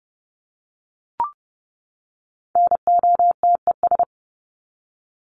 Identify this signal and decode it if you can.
Morse code, heard as a keyed tone, about 30 wpm, 700 Hz. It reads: NOTES
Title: ES:S Suffix - Top 100 Words with Base Words 30wpm